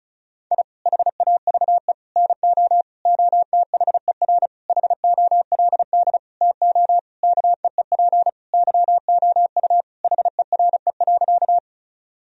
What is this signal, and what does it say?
Morse code, heard as a keyed tone, about 36 wpm, 700 Hz. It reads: I HAVE NO OTHER HOLD TO KEEP YOU HERE.
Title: Sentences from Top 1000 Words - Review Part 2 - 35wpm